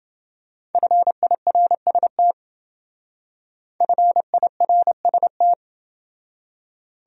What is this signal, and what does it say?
Morse code, heard as a keyed tone, about 28 wpm, 700 Hz. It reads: FIRST FIRST